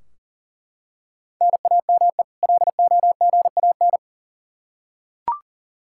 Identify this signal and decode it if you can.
Morse code, heard as a keyed tone, about 40 wpm, 700 Hz. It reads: NAME LOGAN
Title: Practice QSO - Exchange 1 - Beginning - 40wpm